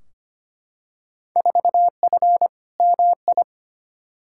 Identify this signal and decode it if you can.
Morse code, heard as a keyed tone, about 25 wpm, 700 Hz. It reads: 4F MI
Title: Field Day Exchanges Encore - 25wpm